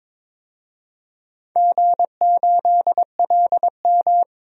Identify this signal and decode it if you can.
Morse code, heard as a keyed tone, about 22 wpm, 700 Hz. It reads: G8LM
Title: DXCC-100 Call Signs - 4 Characters - 22wpm